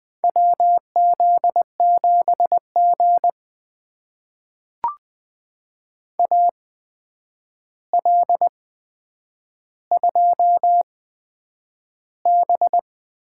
Morse code, heard as a keyed tone, about 20 wpm, spaced as for 9 wpm, 700 Hz. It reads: WZ7G AL2B
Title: USA Call Signs - 4 Characters - 8x Character Spacing - 20wpm